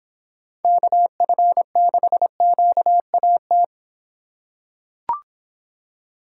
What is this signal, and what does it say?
Morse code, heard as a keyed tone, about 24 wpm, 700 Hz. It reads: KF6QAT